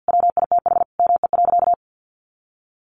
Morse code, heard as a keyed tone, about 50 wpm, 700 Hz. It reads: WITH ME.
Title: Sentences from Top 700 Words 50wpm